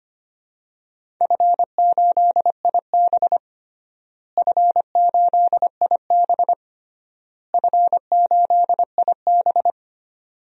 Morse code, heard as a keyed tone, about 25 wpm, 700 Hz. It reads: F8IB F8IB F8IB
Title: DXCC-100 Call Signs - 4 Characters - Repeated 3x - 25wpm